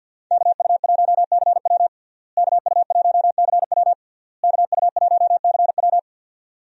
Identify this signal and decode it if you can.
Morse code, heard as a keyed tone, about 52 wpm, 700 Hz. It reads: KU1CW KU1CW KU1CW
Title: USA Call Signs - 5 Characters - Repeated 3x - 50wpm